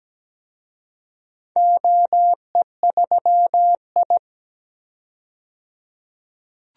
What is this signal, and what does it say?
Morse code, heard as a keyed tone, about 17 wpm, 700 Hz. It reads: OE3I
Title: DXCC-100 Call Signs - 4 Characters - 17wpm